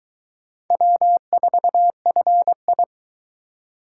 Morse code, heard as a keyed tone, about 22 wpm, 700 Hz. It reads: W4FI